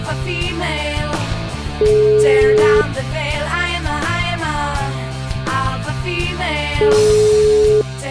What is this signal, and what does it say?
Ringback tone (Europe, 425 Hz), with noise.